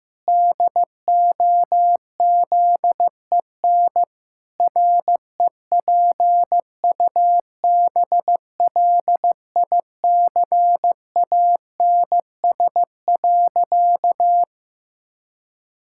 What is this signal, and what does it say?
Morse code, heard as a keyed tone, about 15 wpm, 700 Hz. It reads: DOZEN REPUBLICANS.
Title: Pangram Challenge v2 - 15wpm